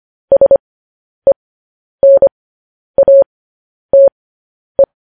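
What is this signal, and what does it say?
Morse code, heard as a keyed tone, about 25 wpm, spaced as for 10 wpm, 550 Hz. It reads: SENATE
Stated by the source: Morse code news headlines sent as a reading transcription, preceded by vvv.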